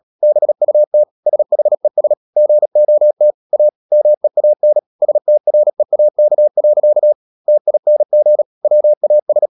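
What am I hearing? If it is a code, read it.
Morse code, heard as a keyed tone, about 37 wpm, 600 Hz. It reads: BUT SHES GOT A MEAN STREAK. TING WAS
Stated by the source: Morse code practice at 35–39 wpm sending English text of Ken Follett's novel Never, from CW med Gnister.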